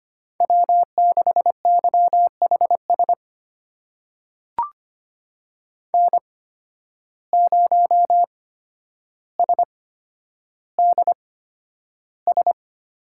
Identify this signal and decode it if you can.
Morse code, heard as a keyed tone, about 25 wpm, spaced as for 11 wpm, 700 Hz. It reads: W6YHS N0SDS
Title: USA Call Signs - 5 Characters - 8x Character Spacing - 25wpm